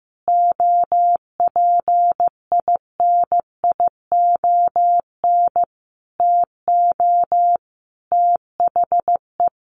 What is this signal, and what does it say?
Morse code, heard as a keyed tone, about 15 wpm, 700 Hz. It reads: OPINION TO THE
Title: Sentences from Top 1300 Words 15wpm